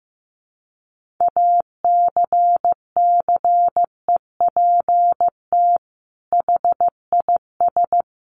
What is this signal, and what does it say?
Morse code, heard as a keyed tone, about 15 wpm, 700 Hz. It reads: ACCEPT HIS
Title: Sentences from Top 1300 Words 15wpm